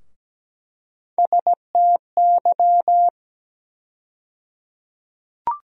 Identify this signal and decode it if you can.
Morse code, heard as a keyed tone, about 17 wpm, 700 Hz. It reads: STY
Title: USA Call Sign Postfixes - 3 Letters 17wpm